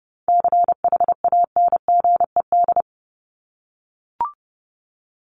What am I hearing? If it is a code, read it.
Morse code, heard as a keyed tone, about 30 wpm, 700 Hz. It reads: CHANGED
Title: ED Suffix - Top 100 Words with Base Words 30wpm